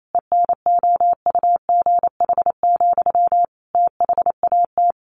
Morse code, heard as a keyed tone, about 28 wpm, 700 Hz. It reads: ENOUGH, THAT
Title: Sentences from Top 800 Words 28wpm